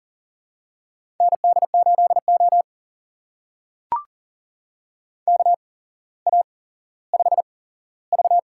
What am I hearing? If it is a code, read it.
Morse code, heard as a keyed tone, about 40 wpm, spaced as for 17 wpm, 700 Hz. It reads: ND8O KA5V
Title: USA Call Signs - 4 Characters - 8x Character Spacing - 40wpm